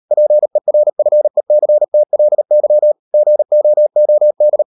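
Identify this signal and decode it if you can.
Morse code, heard as a keyed tone, about 38 wpm, 600 Hz. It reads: PERFECTLY GOOD
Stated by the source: Morse code practice at 38 WPM, sending Ken Follett's novel Never in English, from CW med Gnister.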